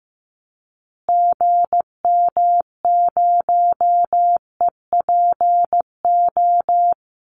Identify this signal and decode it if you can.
Morse code, heard as a keyed tone, about 15 wpm, 700 Hz. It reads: GM0EPO
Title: Call Signs Repeated 2x - 15wpm